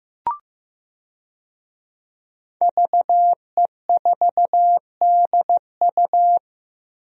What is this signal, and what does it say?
Morse code, heard as a keyed tone, about 15 wpm, 700 Hz. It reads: VE4DU
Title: DXCC-100 Call Signs - 5 Characters - Repeated 2x - 15wpm